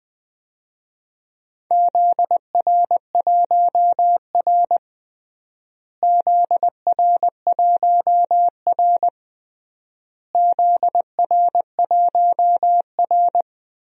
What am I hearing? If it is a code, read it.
Morse code, heard as a keyed tone, about 20 wpm, 700 Hz. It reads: ZR1R ZR1R ZR1R
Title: DXCC-100 Call Signs - 4 Characters - Repeated 3x - 20wpm